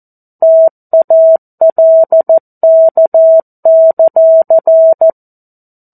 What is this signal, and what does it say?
Morse code, heard as a keyed tone, about 14 wpm, 650 Hz. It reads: TALK;
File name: 02-Chapter-02 14 WPM